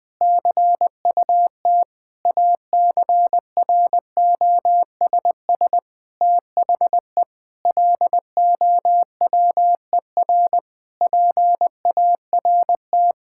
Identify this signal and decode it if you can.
Morse code, heard as a keyed tone, about 20 wpm, 700 Hz. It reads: CUT ACROSS THE LOWER PART
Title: Sentences from Top 1000 Words - Review Part 5 - 20wpm